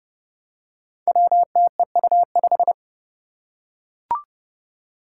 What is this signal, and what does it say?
Morse code, heard as a keyed tone, about 30 wpm, 700 Hz. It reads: WTEU5
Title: Five Challenging Letter-Number Combinations 30wpm